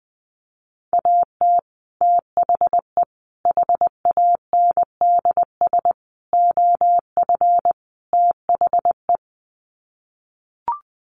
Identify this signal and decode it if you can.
Morse code, heard as a keyed tone, about 20 wpm, 700 Hz. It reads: AT THE HANDS OF THE
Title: Sets of 5 Words Encore2 20wpm